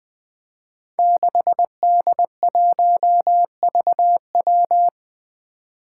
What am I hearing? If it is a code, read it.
Morse code, heard as a keyed tone, about 20 wpm, 700 Hz. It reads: 6D1VW
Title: Five Challenging Letter-Number Combinations 20wpm